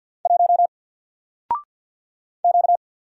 Morse code, heard as a keyed tone, about 50 wpm, 700 Hz. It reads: 1 Q
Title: Single Letter-Number 50wpm